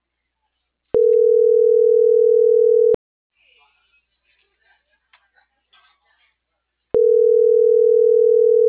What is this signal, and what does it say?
Ringback tone (North America), 440 + 480 Hz.